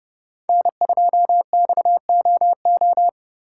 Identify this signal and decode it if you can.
Morse code, heard as a keyed tone, about 32 wpm, 700 Hz. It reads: N2XOO
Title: USA Call Signs - 5 Characters - 2x Character Spacing - 30wpm